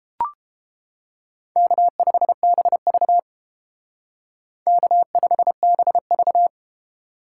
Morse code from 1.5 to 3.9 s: K5BV

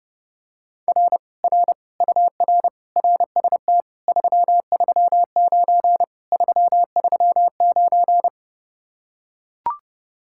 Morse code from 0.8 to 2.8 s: R R UR

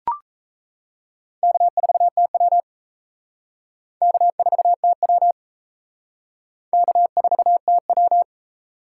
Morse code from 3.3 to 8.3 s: K4TW K4TW

